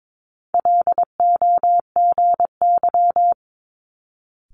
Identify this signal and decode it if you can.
Morse code, heard as a keyed tone, about 22 wpm, 700 Hz. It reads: LOGY
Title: Multi-letter Phonemes Quick Fire 22wpm